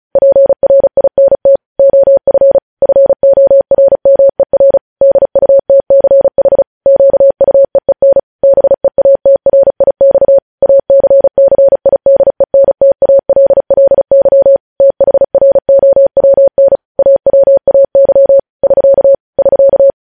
Morse code, heard as a keyed tone, about 36 wpm, 550 Hz. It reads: PRINT OF FORMER DUTCH QUEEN BEATRIX ACCIDENTALLY THROWN AWAY <SK> <SK>
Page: News headlines in Morse code at 35 wpm